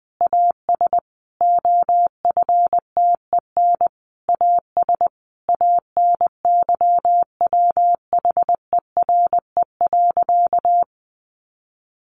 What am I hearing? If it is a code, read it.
Morse code, heard as a keyed tone, about 20 wpm, 700 Hz. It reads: AS OFTEN AS ANYWHERE.
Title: Sentences from Top 1400 Words 20wpm